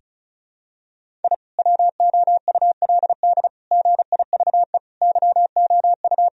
Morse code, heard as a keyed tone, about 34 wpm, 700 Hz. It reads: I WOULD GIVE YOU
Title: Sentences from Top 1000 Words - Review Part 1 - 35wpm